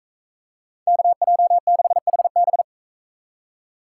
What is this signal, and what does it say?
Morse code, heard as a keyed tone, about 44 wpm, 700 Hz. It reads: KJ6HB